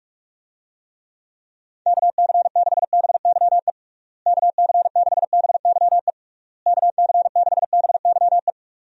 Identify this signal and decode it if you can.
Morse code, heard as a keyed tone, about 46 wpm, 700 Hz. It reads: KC6BYE KC6BYE KC6BYE
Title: USA Call Signs - 6 Characters - Repeated 3x - 45wpm